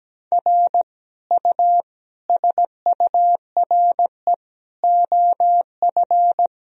Morse code, heard as a keyed tone, about 17 wpm, 700 Hz. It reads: R U SURE OF
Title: Sentences from Top 300 Words and Abbreviations 17wpm